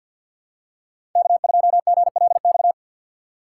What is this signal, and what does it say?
Morse code, heard as a keyed tone, about 52 wpm, 700 Hz. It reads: K2GLX
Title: USA Call Signs - 6x Character Spacing - 50wpm